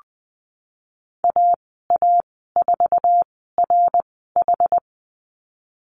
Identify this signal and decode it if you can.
Morse code, heard as a keyed tone, about 20 wpm, spaced as for 14 wpm, 700 Hz. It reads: AA4RH
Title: 128 Lesson 21g - Number 6 - Call Signs 20wpm